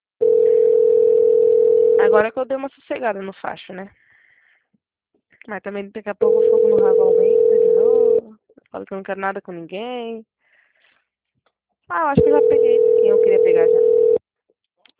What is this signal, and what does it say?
Ringback tone (North America), 440 + 480 Hz, with speech.
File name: out- -2010- - - .351